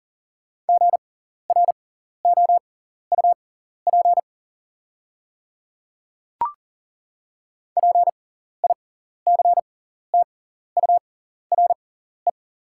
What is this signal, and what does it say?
Morse code, heard as a keyed tone, about 40 wpm, spaced as for 14 wpm, 690 Hz. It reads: GROUP PICTURE
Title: 6x Character Spacing - Top 500 Words - 40wpm